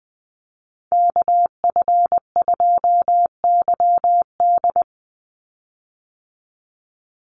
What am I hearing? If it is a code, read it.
Morse code, heard as a keyed tone, about 20 wpm, 700 Hz. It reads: KF2YD
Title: USA Call Signs 20wpm